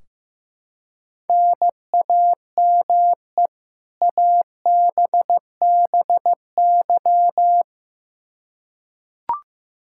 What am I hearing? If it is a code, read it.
Morse code, heard as a keyed tone, about 15 wpm, 700 Hz. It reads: NAME ABBY
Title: Practice QSO - Exchange 1 - Beginning - 15wpm